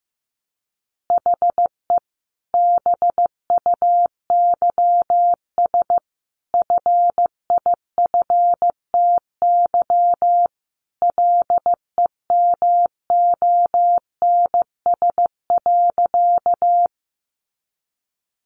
Morse code, heard as a keyed tone, about 15 wpm, 700 Hz. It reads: HE BUYS FIFTY LEMONS.
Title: Additional Easy Phrases 15wpm